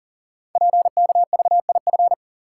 Morse code, heard as a keyed tone, about 39 wpm, 690 Hz. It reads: PKVIF
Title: Code Groups - 5 Letters - 40wpm